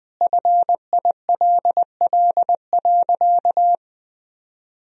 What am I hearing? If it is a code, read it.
Morse code, heard as a keyed tone, about 20 wpm, 700 Hz. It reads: FILL.
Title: Sentences from Top 1000 Words - Review Part 3 - 20wpm